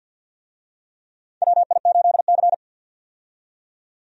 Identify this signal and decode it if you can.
Morse code, heard as a keyed tone, about 47 wpm, 700 Hz. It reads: WI8C
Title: USA Call Signs - 4 Characters - 50wpm